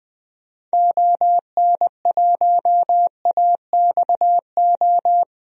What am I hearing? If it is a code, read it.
Morse code, heard as a keyed tone, about 20 wpm, 700 Hz. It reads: ON1AXO
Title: DXCC-100 Call Signs - 6 Characters - 20wpm